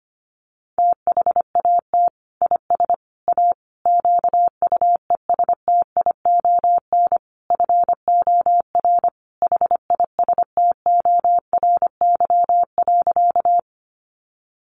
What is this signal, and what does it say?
Morse code, heard as a keyed tone, about 25 wpm, 700 Hz. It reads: THAT IS A QUESTION FOR HISTORY.
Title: Sentences from Top 600 Words 25wpm